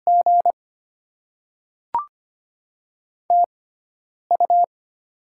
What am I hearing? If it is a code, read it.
Morse code, heard as a keyed tone, about 25 wpm, spaced as for 9 wpm, 700 Hz. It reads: G TU
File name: Top 20 QSO Elements - 6x Character Spacing - 25wpm